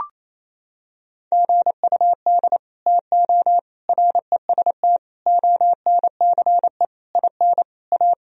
Morse code, heard as a keyed tone, about 28 wpm, 700 Hz. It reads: GUD TO REST ONCE IN A
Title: Sentences from Top 500 Words and Abbreviations 28wpm